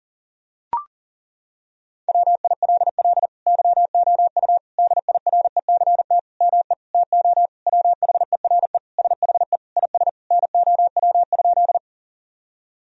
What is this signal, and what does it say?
Morse code, heard as a keyed tone, about 40 wpm, 700 Hz. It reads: WILL YOU DIRECT ME TO WHERE SHE IS NOW?
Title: Sentences from Top 1000 Words - Review Part 2 - 40wpm